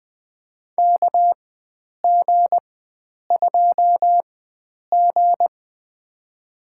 Morse code, heard as a keyed tone, about 20 wpm, spaced as for 9 wpm, 700 Hz. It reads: KG2G
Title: USA Call Signs - 4x Character Spacing - 20wpm